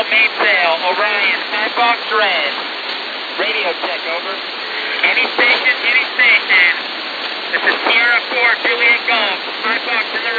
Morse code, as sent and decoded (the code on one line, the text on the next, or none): none